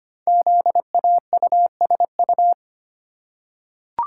Morse code, heard as a keyed tone, about 25 wpm, 700 Hz. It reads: ZAUSU
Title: Code Groups - 5 Letters - 25wpm